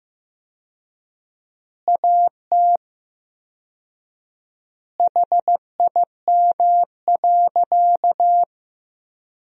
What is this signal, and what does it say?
Morse code, heard as a keyed tone, about 16 wpm, 700 Hz. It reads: AT HIM.
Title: 4x Word-Spacing - Sentences from Top 100 Words - 15wpm